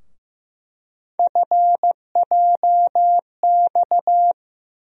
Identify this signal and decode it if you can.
Morse code, heard as a keyed tone, about 15 wpm, 700 Hz. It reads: FJX
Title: USA Call Sign Postfixes - 3 Letters 15wpm